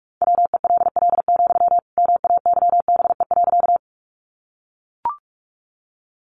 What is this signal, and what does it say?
Morse code, heard as a keyed tone, about 44 wpm, 700 Hz. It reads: WELL, MAYBE.
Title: Sentences from Top 1000 Words 45wpm